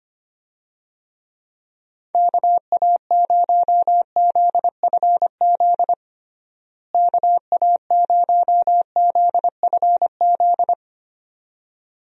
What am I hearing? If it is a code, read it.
Morse code, heard as a keyed tone, about 25 wpm, 700 Hz. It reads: KA0ZFZ KA0ZFZ
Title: USA Call Signs - 6 Characters - Repeated 2x - 25wpm